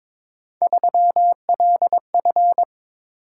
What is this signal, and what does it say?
Morse code, heard as a keyed tone, about 26 wpm, 700 Hz. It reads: 3LF
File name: Speed-Racing - Three Letter-Number Combinations - 22wpm